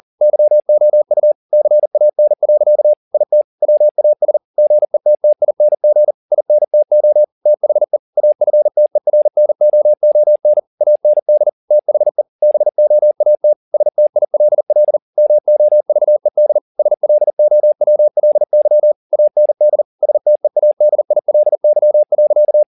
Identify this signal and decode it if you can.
Morse code, heard as a keyed tone, about 40 wpm, 600 Hz. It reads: YOU CAN. IT WAS GETTING INTO THE AFTERNOON AND THE BOAT STILL MOVED SLOWLY AND STEADILY.